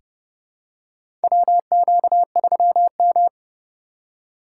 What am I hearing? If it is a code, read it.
Morse code, heard as a keyed tone, about 28 wpm, 700 Hz. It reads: WQ3M
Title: USA Call Signs - 4 Characters - 2x Character Spacing - 30wpm